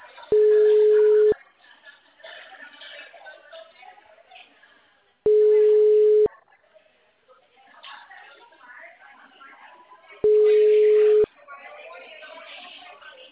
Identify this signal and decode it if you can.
Ringback tone (Europe, 425 Hz), with speech.
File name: out- -2007- - - .210